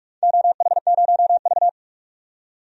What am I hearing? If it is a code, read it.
Morse code, heard as a keyed tone, about 45 wpm, 700 Hz. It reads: OH0V
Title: DXCC-100 Call Signs - 4 Characters - 45wpm